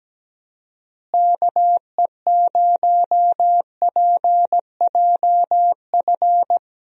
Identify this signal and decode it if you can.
Morse code, heard as a keyed tone, about 17 wpm, 700 Hz. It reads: KE0PJF
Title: USA Call Signs - 6 Characters - Repeated 3x - 17wpm